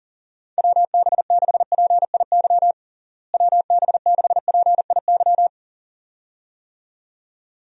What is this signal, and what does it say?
Morse code, heard as a keyed tone, about 40 wpm, 700 Hz. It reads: WB6PIY WB6PIY
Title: USA Call Signs - 6 Characters - Repeated 2x - 40wpm